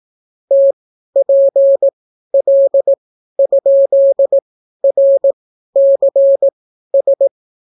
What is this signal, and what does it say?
Morse code, heard as a keyed tone, about 18 wpm, spaced as for 12 wpm, 550 Hz. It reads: TPL?RCS